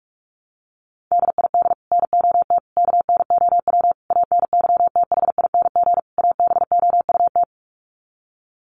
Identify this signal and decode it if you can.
Morse code, heard as a keyed tone, about 44 wpm, 700 Hz. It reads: DID NOT KNOW ANYTHING ABOUT
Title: Sets of 5 Words Encore2 45wpm